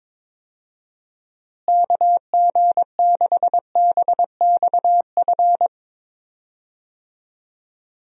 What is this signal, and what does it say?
Morse code, heard as a keyed tone, about 22 wpm, 700 Hz. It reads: KG6BXF
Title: USA Call Signs - 6 Characters - Repeated 3x - 22wpm